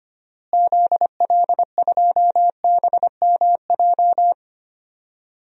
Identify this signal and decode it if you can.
Morse code, heard as a keyed tone, about 25 wpm, 700 Hz. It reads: ZL2BMJ
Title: DXCC-100 Call Signs - 6 Characters - Repeated 3x - 25wpm